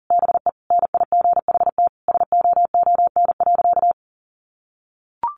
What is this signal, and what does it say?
Morse code, heard as a keyed tone, about 40 wpm, 700 Hz. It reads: BE NIGHT SOON.
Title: Sentences from Top 1300 Words 40wpm